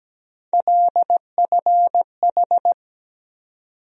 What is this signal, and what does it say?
Morse code, heard as a keyed tone, about 17 wpm, 700 Hz. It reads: LFH
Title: USA Call Sign Postfixes - 3 Letters 17wpm